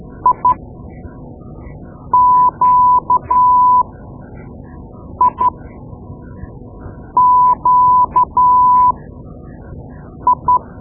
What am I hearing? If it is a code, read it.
Morse code, heard as a keyed tone, about 12 wpm, spaced as for 5 wpm, 1000 Hz. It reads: IQIQI
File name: Morse Lessons / Morse Lessons / Tape-2B